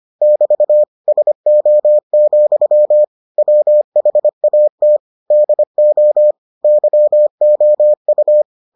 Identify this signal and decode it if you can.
Morse code, heard as a keyed tone, about 25 wpm, 600 Hz. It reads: <BT> SO, WHAT DO YOU